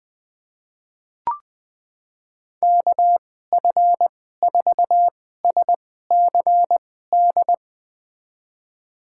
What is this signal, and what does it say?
Morse code, heard as a keyed tone, about 20 wpm, spaced as for 14 wpm, 700 Hz. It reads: KF4SCD
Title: USA Call Signs - 6 Characters - 2x Character Spacing - 20wpm